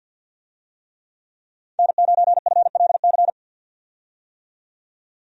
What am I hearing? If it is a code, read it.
Morse code, heard as a keyed tone, about 47 wpm, 700 Hz. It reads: N9FLC
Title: USA Call Signs - 5 Characters - 50wpm